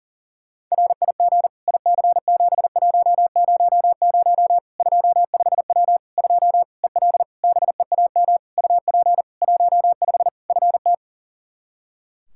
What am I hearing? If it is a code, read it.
Morse code, heard as a keyed tone, about 40 wpm, 700 Hz. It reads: RIG IC7100 25W 2 EL BEAM UP 15 FT
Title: K1USN SST QSO Practice 40wpm